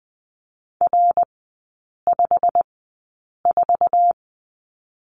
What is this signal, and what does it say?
Morse code, heard as a keyed tone, about 20 wpm, spaced as for 8 wpm, 690 Hz. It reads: R54
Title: 087 Lesson 15e - Number 5 - Three Characters 20wpm